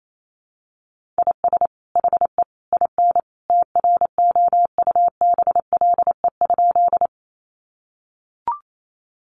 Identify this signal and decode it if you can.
Morse code, heard as a keyed tone, about 28 wpm, 700 Hz. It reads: IS HE IN TROUBLE?
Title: Sentences from Top 1200 Words 28wpm